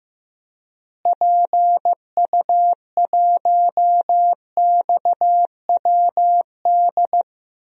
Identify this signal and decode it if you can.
Morse code, heard as a keyed tone, about 15 wpm, 700 Hz. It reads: PU1XWD
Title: DXCC-100 Call Signs - 6 Characters - Repeated 2x - 15wpm